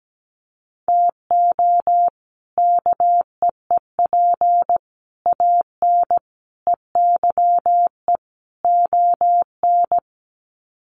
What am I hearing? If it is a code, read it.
Morse code, heard as a keyed tone, about 17 wpm, 700 Hz. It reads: TO KEEP AN EYE ON
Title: Sets of 5 Words Encore 17wpm